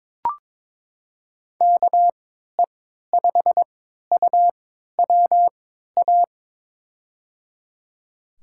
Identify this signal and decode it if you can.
Morse code, heard as a keyed tone, about 22 wpm, spaced as for 12 wpm, 700 Hz. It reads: KE5UWA
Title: USA Call Signs - 6 Characters - 3x Character Spacing - 22wpm